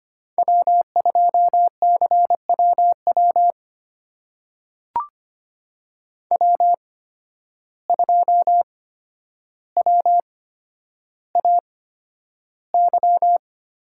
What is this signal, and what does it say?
Morse code, heard as a keyed tone, about 25 wpm, spaced as for 11 wpm, 700 Hz. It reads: W2CWW W2WAY
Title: USA Call Signs - 5 Characters - 8x Character Spacing - 25wpm